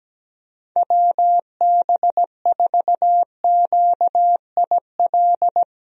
Morse code, heard as a keyed tone, about 17 wpm, 700 Hz. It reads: WB4QIL
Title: USA Call Signs - 6 Characters - Repeated 3x - 17wpm